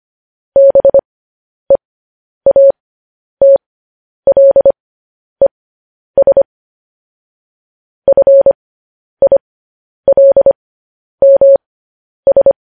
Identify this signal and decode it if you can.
Morse code, heard as a keyed tone, about 25 wpm, spaced as for 10 wpm, 550 Hz. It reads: BEATLES FILMS
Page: News Headlines in Morse Code at 10 WPM